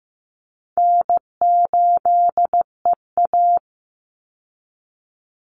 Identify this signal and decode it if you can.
Morse code, heard as a keyed tone, about 15 wpm, 700 Hz. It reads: N8EA
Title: Call Signs Repeated 2x - 15wpm